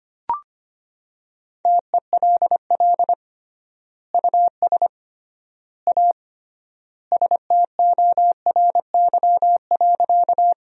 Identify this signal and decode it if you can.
Morse code, heard as a keyed tone, about 25 wpm, 700 Hz. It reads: TELL US A STORY.
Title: 3x Word-Spacing - Sentences from Top 300 Words - 25wpm